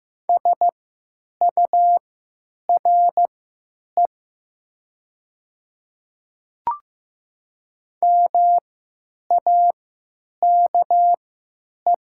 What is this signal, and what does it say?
Morse code, heard as a keyed tone, about 15 wpm, spaced as for 9 wpm, 700 Hz. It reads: SURE MAKE
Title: 3x Character Spacing - Top 300 Words - 15wpm